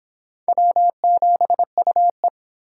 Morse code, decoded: W7UE

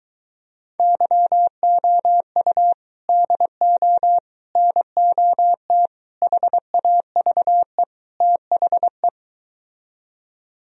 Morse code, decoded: YOU DO NOT HAVE THE

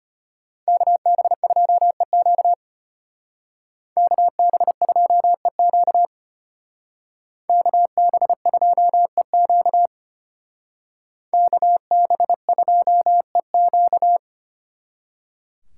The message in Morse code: KB2EQ KB2EQ KB2EQ KB2EQ

